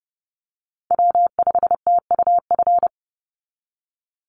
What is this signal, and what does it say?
Morse code, heard as a keyed tone, about 30 wpm, 700 Hz. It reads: W5TUF
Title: Call Signs - 30wpm